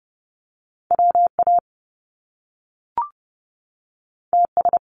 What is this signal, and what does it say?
Morse code, heard as a keyed tone, about 30 wpm, 700 Hz. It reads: WA TS
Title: Top Two Letter Combinations in Words 30wpm